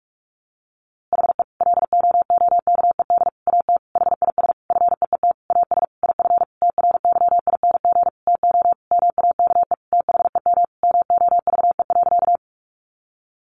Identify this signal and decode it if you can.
Morse code, heard as a keyed tone, about 44 wpm, 700 Hz. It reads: HE LOOKED AT HIS FEET AS IF TRYING TO MAKE THEM MOVE.
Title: A Sound Of Thunder 45wpm